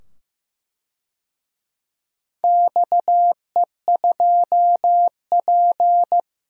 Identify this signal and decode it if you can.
Morse code, heard as a keyed tone, about 15 wpm, 700 Hz. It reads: XE2P
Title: DXCC-100 Call Signs - 4 Characters - 15wpm